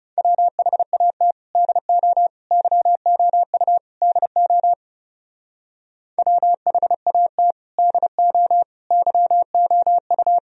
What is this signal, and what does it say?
Morse code, heard as a keyed tone, about 38 wpm, 700 Hz. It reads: WHAT DO YOU DO WHAT DO YOU